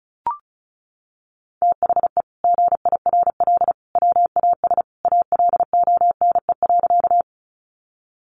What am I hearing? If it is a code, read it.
Morse code, heard as a keyed tone, about 34 wpm, 700 Hz. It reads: THE GIRL WAS ALONE.
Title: Sentences from Top 1000 Words 35wpm